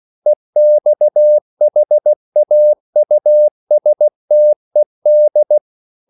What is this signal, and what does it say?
Morse code, heard as a keyed tone, about 16 wpm, 600 Hz. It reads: EXHAUSTED
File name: CW / Practice / EnglishQuotations0416